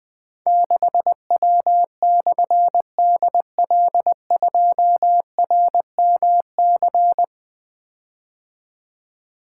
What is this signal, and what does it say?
Morse code, heard as a keyed tone, about 21 wpm, 700 Hz. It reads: 6W/DL2RMC
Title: Difficult Call Signs 20wpm